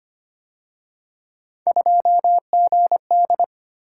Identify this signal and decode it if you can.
Morse code, heard as a keyed tone, about 29 wpm, 700 Hz. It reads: 2GD
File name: Speed-Racing - Three Letter-Number Combinations - 25wpm